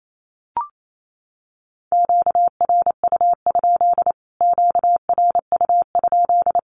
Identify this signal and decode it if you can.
Morse code, heard as a keyed tone, about 28 wpm, 700 Hz. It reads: QRU? QRU?
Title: Q-Codes Repeated 3x 28wpm